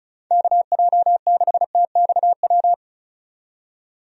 Morse code, heard as a keyed tone, about 34 wpm, 700 Hz. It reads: KJ6TXW
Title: USA Call Signs - 6 Characters - 35wpm